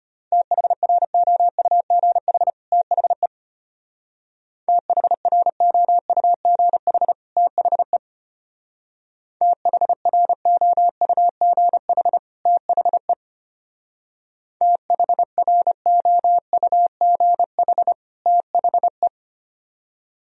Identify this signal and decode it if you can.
Morse code, heard as a keyed tone, about 30 wpm, 700 Hz. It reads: THROUGH THE THROUGH THE THROUGH THE THROUGH THE
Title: Speed-Racing - Sets of 2 Words 25wpm